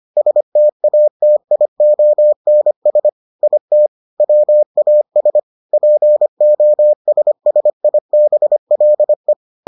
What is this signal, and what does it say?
Morse code, heard as a keyed tone, about 25 wpm, 600 Hz. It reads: STATIONS IT WAS POSSIBLE